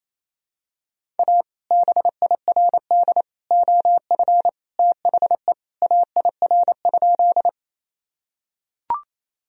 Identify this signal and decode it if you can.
Morse code, heard as a keyed tone, about 28 wpm, 700 Hz. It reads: A BIRD OF THE AIR?
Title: Sentences from Top 1000 Words - Review Part 2 - 28wpm